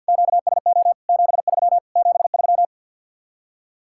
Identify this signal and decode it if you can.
Morse code, heard as a keyed tone, about 64 wpm, 700 Hz. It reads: QSO 73 73